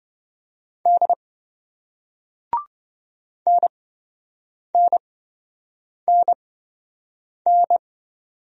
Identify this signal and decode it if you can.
Morse code, heard as a keyed tone, about 27 wpm, spaced as for 7 wpm, 700 Hz. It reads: D NNNN